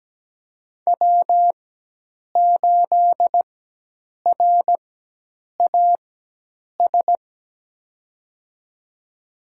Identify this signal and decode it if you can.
Morse code, heard as a keyed tone, about 17 wpm, spaced as for 8 wpm, 700 Hz. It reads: W8RAS
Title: USA Call Signs - 5 Characters - 4x Character Spacing - 17wpm